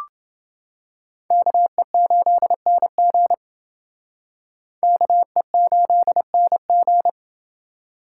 Morse code, heard as a keyed tone, about 29 wpm, 700 Hz. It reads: KE8NG KE8NG